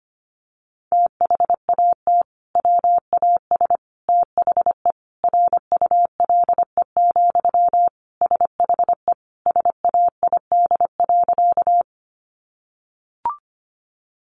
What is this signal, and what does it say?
Morse code, heard as a keyed tone, about 25 wpm, 700 Hz. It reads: THAT WAS THE RULE, SHE SAID.
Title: Sentences from Top 400 Words 25wpm